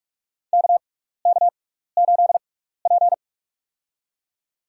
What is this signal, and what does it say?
Morse code, heard as a keyed tone, about 46 wpm, spaced as for 15 wpm, 700 Hz. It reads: KK8P
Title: USA Call Signs - 4 Characters - 6x Character Spacing - 45wpm